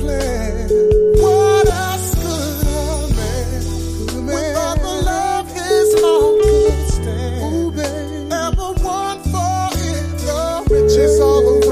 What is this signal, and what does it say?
Ringback tone (Europe, 425 Hz), with noise.